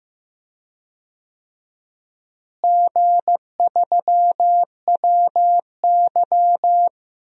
Morse code, heard as a keyed tone, about 15 wpm, 700 Hz. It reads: G3WY
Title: DXCC-100 Call Signs - 4 Characters - Repeated 3x - 15wpm